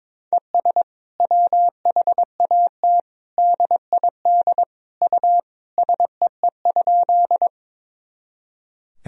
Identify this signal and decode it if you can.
Morse code, heard as a keyed tone, about 22 wpm, 700 Hz. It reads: ES WHAT DID U SEE?
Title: Sentences from Top 100 Words and Abbreviations 22wpm